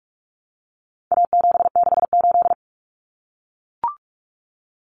Morse code, heard as a keyed tone, about 44 wpm, 700 Hz. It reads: A768
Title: Common Cut Numbers - 4-Digit 45wpm